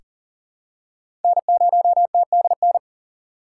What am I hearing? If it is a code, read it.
Morse code, heard as a keyed tone, about 40 wpm, 700 Hz. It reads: N0TDN
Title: USA Call Signs - 6x Character Spacing - 40wpm